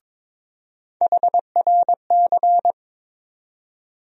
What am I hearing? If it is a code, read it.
Morse code, heard as a keyed tone, about 21 wpm, 700 Hz. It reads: HRC
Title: USA Call Sign Postfixes - 3 Letters 22wpm